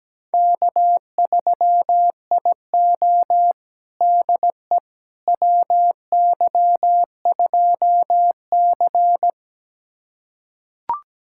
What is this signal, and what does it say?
Morse code, heard as a keyed tone, about 17 wpm, 700 Hz. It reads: K3IO DE WY2C
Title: USA Callsign DE Callsign - 4 Characters - 17wpm